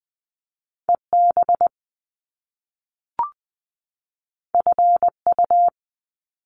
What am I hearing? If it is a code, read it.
Morse code, heard as a keyed tone, about 20 wpm, 700 Hz. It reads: EB FU
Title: Top Two Letter Combinations in Words 20wpm